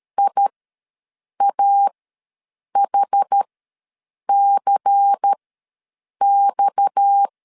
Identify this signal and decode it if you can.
Morse code, heard as a keyed tone, about 13 wpm, spaced as for 7 wpm, 800 Hz. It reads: IAHCX
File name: LEZIONI / 8 YJ45 / 8 Y ws 12wpm 3409